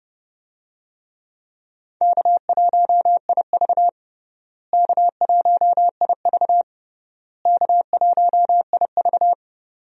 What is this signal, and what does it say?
Morse code, heard as a keyed tone, about 30 wpm, 700 Hz. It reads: K1IV K1IV K1IV
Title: USA Call Signs - 4 Characters - Repeated 3x - 30wpm